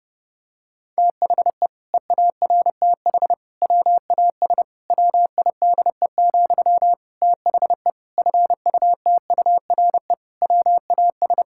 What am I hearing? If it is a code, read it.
Morse code, heard as a keyed tone, about 30 wpm, 700 Hz. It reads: THE EARTH WAS WIDE, THE FUTURE WAS
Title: Sentences from Top 1000 Words - Review Part 6 - 30wpm